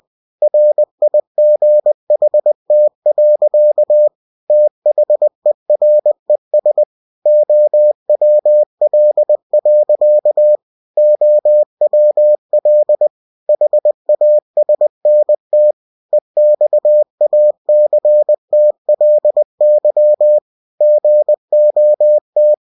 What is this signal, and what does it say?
Morse code, heard as a keyed tone, about 20 wpm, 600 Hz. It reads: RIGHT. THERES OWL. OWL HASNT EXACTLY GOT
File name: cw-books / winnie-the-pooh / winnie-20wpm-page-0091